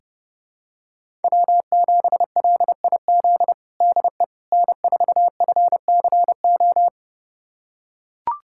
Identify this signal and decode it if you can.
Morse code, heard as a keyed tone, about 32 wpm, 700 Hz. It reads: W7LIZ DE N4FCO
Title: USA Callsign DE Callsign - 5 Characters - 30wpm